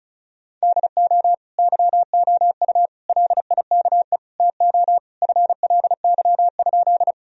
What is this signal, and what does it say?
Morse code, heard as a keyed tone, about 36 wpm, 700 Hz. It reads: DO YOU LIKE TO FLY?
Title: Sentences from Top 1000 Words - Review Part 2 - 35wpm